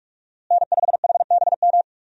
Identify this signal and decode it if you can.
Morse code, heard as a keyed tone, about 44 wpm, 700 Hz. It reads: N5HBM
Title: USA Call Signs - 5 Characters - Repeated 3x - 45wpm